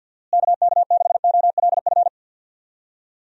Morse code, heard as a keyed tone, about 49 wpm, 700 Hz. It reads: KK6OLF
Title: USA Call Signs - 6 Characters - Repeated 3x - 50wpm